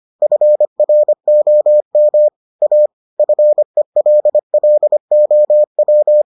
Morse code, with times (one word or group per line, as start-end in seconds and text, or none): 0.2-2.3 FROM
2.6-2.9 A
3.2-6.2 FELLOW